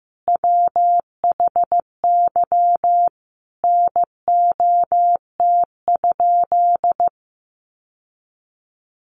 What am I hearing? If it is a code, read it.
Morse code, heard as a keyed tone, about 15 wpm, 700 Hz. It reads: WHY NOT?
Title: Sentences from Top 600 Words 15wpm